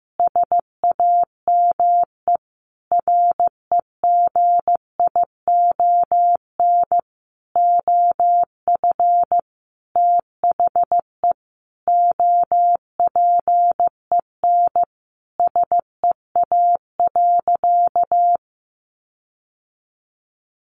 Morse code, heard as a keyed tone, about 15 wpm, 700 Hz. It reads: SAME REGION OF THE OPEN SEA.
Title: Sentences from Top 1500 Words 15wpm